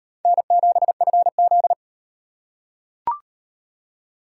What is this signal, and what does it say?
Morse code, heard as a keyed tone, about 38 wpm, 700 Hz. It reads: N7FZ